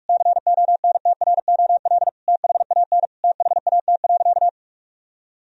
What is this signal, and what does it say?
Morse code, heard as a keyed tone, about 47 wpm, 700 Hz. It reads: CONTROL THAN THAT.
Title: Sentences from Top 1000 Words - Review Part 3 - 45wpm